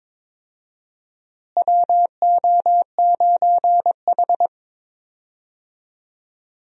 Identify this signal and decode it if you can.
Morse code, heard as a keyed tone, about 22 wpm, 700 Hz. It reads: WO9H
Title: USA Call Signs - 4 Characters - Repeated 3x - 22wpm